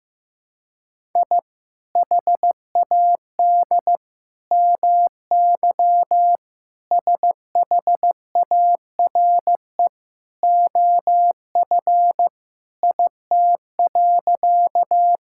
Morse code, heard as a keyed tone, about 15 wpm, 700 Hz. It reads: I HAD MY SHARE OF IT.
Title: Sentences from Top 1000 Words - Review Part 6 - 15wpm